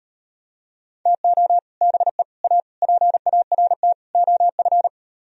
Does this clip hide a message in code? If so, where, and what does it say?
Yes: Morse code from 1.1 to 4.9 s, heard as a keyed tone, about 38 wpm, 700 Hz. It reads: TO BE A PART OF